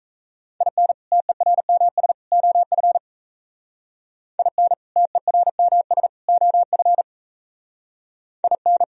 Morse code, heard as a keyed tone, about 36 wpm, 700 Hz. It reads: IN TERMS OF IN TERMS OF IN